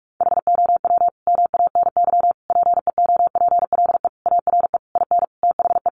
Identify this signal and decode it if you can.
Morse code, heard as a keyed tone, about 44 wpm, 700 Hz. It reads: HOW MANY PEOPLE ARE IN THE
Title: Sentences from Top 1200 Words 45wpm